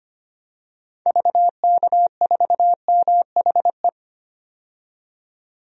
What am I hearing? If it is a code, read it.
Morse code, heard as a keyed tone, about 25 wpm, 700 Hz. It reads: VK4MHE
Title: DXCC-100 Call Signs - 6 Characters - 25wpm